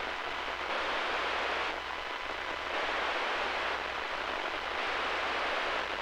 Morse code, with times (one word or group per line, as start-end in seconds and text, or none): none